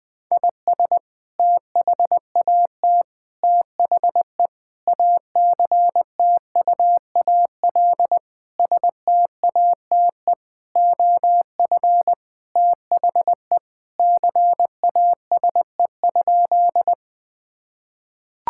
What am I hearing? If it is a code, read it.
Morse code, heard as a keyed tone, about 20 wpm, 700 Hz. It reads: IS THAT THE ACTUAL STATE OF THE CASE?
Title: Sentences from Top 1000 Words - Review Part 5 - 20wpm